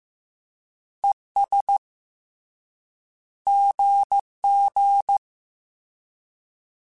Morse code, heard as a keyed tone, about 15 wpm, 800 Hz. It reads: ES GG
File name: Misc / Misc-mp3s / Words @13